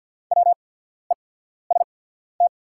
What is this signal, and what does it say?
Morse code, heard as a keyed tone, about 48 wpm, spaced as for 14 wpm, 690 Hz. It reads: WEST
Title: 8x Character Spacing - Top 500 Words - 50wpm